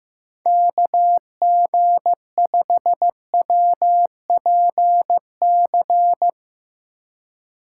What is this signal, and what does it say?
Morse code, heard as a keyed tone, about 15 wpm, 700 Hz. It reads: KG5WPC
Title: USA Call Signs - 4x Character Spacing - 15wpm